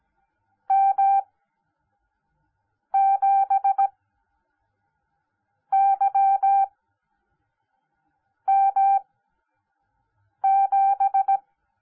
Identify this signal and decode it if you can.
Morse code, heard as a keyed tone, about 17 wpm, spaced as for 5 wpm, 790 Hz. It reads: M7YM7
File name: Onlinelehrgang / morsen / DJ4UF Morsen10